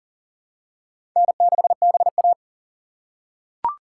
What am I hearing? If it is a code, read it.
Morse code, heard as a keyed tone, about 40 wpm, 700 Hz. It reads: N6BA
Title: USA Call Signs - 4 Characters - 6x Character Spacing - 40wpm